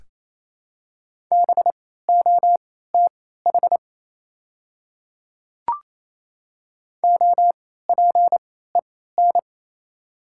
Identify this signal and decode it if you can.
Morse code, heard as a keyed tone, about 28 wpm, spaced as for 16 wpm, 700 Hz. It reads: BOTH OPEN
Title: 3x Character Spacing - Top 500 Words - 28wpm